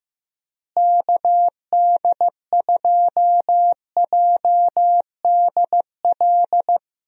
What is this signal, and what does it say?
Morse code, heard as a keyed tone, about 15 wpm, 700 Hz. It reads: KD2JDL
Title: USA Call Signs - 6 Characters - Repeated 3x - 15wpm